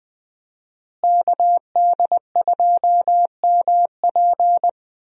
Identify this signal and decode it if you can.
Morse code, heard as a keyed tone, about 20 wpm, 700 Hz. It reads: KD2MP
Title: USA Call Signs - 5 Characters - Repeated 2x - 20wpm